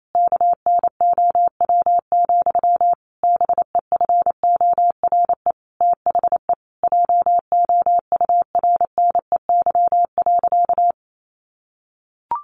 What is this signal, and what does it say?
Morse code, heard as a keyed tone, about 28 wpm, 700 Hz. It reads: KNOW, BEFORE THE JOURNEY.
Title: A Sound Of Thunder 28wpm